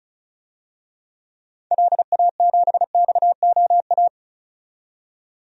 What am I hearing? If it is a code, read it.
Morse code, heard as a keyed tone, about 36 wpm, 700 Hz. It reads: LA7XOA
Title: DXCC-100 Call Signs - 6 Characters - 35wpm